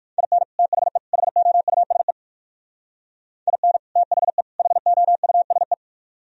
Morse code, heard as a keyed tone, about 63 wpm, 700 Hz. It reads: IN THE HOUSE IN THE HOUSE